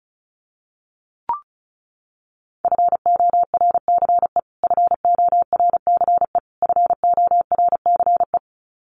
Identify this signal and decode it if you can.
Morse code, heard as a keyed tone, about 34 wpm, 700 Hz. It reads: FORCE FORCE FORCE
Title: Top 500 Words Repeated 35wpm